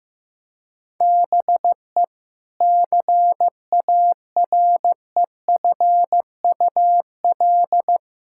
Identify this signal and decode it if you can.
Morse code, heard as a keyed tone, about 15 wpm, 700 Hz. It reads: BE CAREFUL
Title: Sentences from Top 1600 Words 15wpm